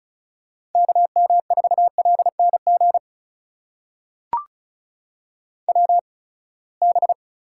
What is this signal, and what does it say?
Morse code, heard as a keyed tone, about 36 wpm, spaced as for 24 wpm, 700 Hz. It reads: KM4LNG WB
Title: USA Call Signs - 6 Characters - 8x Character Spacing - 35wpm